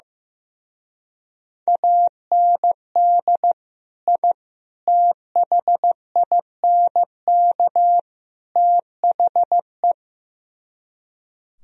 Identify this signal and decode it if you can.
Morse code, heard as a keyed tone, about 15 wpm, 700 Hz. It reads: AND I THINK THE
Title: Speed-Racing - Sets of 4 Words 15wpm